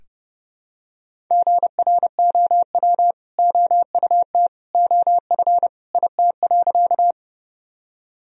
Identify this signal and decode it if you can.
Morse code, heard as a keyed tone, about 30 wpm, 700 Hz. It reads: GROW OUT OF IT.
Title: Sentences from Top 1000 Words - Review Part 1 - 30wpm